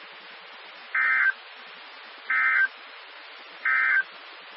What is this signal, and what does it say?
Emergency Alert System (SAME) tones, heard as harsh buzzing bursts. It is heard from 0.8 to 4.1 s.
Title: Nassau and Suffolk County Amateur Repeaters